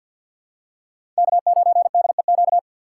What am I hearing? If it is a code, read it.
Morse code, heard as a keyed tone, about 46 wpm, 690 Hz. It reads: K9DEQ
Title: USA Call Signs - 5 Characters - 3x Character Spacing - 50wpm